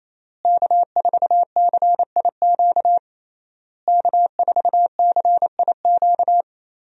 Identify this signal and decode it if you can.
Morse code, heard as a keyed tone, about 28 wpm, 700 Hz. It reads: K4CIQ K4CIQ
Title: USA Call Signs - 5 Characters - Repeated 3x - 28wpm